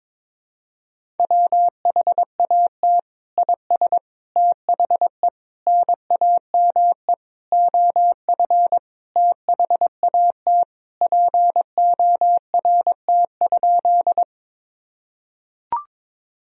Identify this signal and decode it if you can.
Morse code, heard as a keyed tone, about 22 wpm, 700 Hz. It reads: WHAT IS THE NAME OF THAT PORT?
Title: Sentences from Top 1000 Words - Review Part 1 - 22wpm